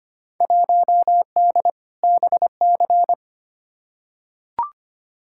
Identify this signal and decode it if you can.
Morse code, heard as a keyed tone, about 25 wpm, 700 Hz. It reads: 1D BC
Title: Field Day Exchanges - 25wpm